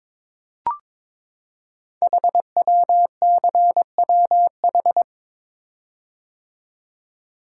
Morse code, heard as a keyed tone, about 22 wpm, 700 Hz. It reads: HWCWH
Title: Code Groups - 5 Letters - 22wpm